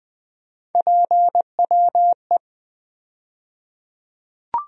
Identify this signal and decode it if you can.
Morse code, heard as a keyed tone, about 21 wpm, 700 Hz. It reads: PWE
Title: USA Call Sign Postfixes - 3 Letters 20wpm